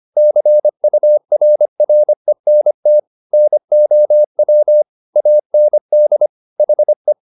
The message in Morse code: CURRENT NOW AND HE

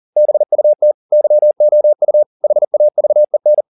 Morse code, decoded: BUT YOU HAVEN

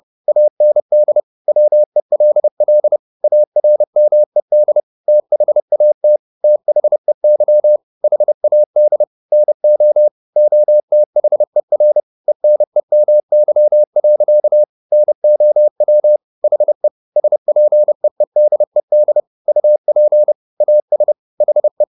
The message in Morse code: AND WELL ARMED THAT THEY HAD NO OTHER ENEMY. NOW HE SPEEDED UP AS HE

